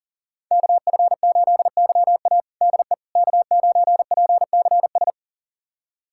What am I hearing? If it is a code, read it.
Morse code, heard as a keyed tone, about 40 wpm, 700 Hz. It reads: KF8YA DE K9PCS
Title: USA Callsign DE Callsign - 5 Characters - 40wpm